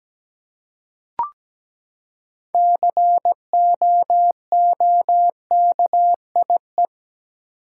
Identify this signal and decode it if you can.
Morse code, heard as a keyed tone, about 17 wpm, 700 Hz. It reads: COOKIE
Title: Six Letter Words - 17wpm